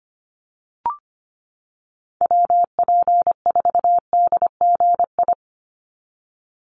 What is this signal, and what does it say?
Morse code, heard as a keyed tone, about 25 wpm, 700 Hz. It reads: WP4DGI
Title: USA Call Signs 25wpm